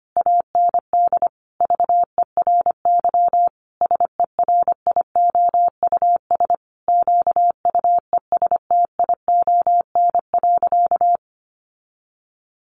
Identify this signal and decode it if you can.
Morse code, heard as a keyed tone, about 25 wpm, 700 Hz. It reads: AND VERY SERIOUS QUESTION.
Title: Sentences from Top 1400 Words 25wpm